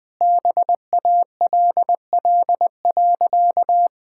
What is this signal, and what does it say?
Morse code, heard as a keyed tone, about 20 wpm, 700 Hz. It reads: BALL.
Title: Easy Sentences with Hints - 20wpm